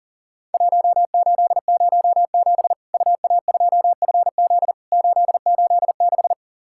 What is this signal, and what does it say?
Morse code, heard as a keyed tone, about 40 wpm, 700 Hz. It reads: 1807 UA2FZ 886
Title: Worked All Europe DX Contest - QTC Exchange - 40wpm